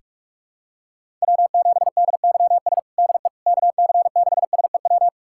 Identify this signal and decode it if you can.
Morse code, heard as a keyed tone, about 46 wpm, 700 Hz. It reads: W7DYS DE KC6SEW
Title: USA Callsign DE Callsign 45wpm